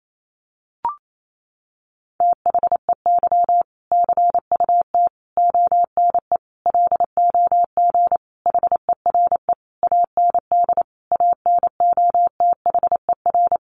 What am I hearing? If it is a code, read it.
Morse code, heard as a keyed tone, about 28 wpm, 700 Hz. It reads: THEY CUT ONE LOG HERE AND ANOTHER
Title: Sentences from Top 1400 Words 28wpm